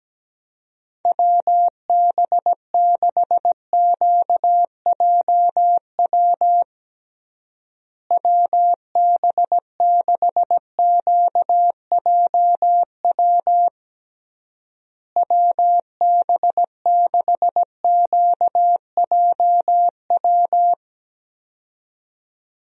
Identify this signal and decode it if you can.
Morse code, heard as a keyed tone, about 17 wpm, 700 Hz. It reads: WB6QJW WB6QJW WB6QJW
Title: USA Call Signs - 6 Characters - Repeated 3x - 17wpm